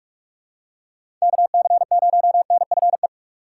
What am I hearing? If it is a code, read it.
Morse code, heard as a keyed tone, about 46 wpm, 700 Hz. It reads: KC0NFE
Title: USA Call Signs - 6 Characters - 6x Character Spacing - 45wpm